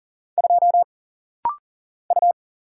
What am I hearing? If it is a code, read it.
Morse code, heard as a keyed tone, about 40 wpm, 700 Hz. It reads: 2 U
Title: Single Letter-Number Alt - Rapid-Fire 40wpm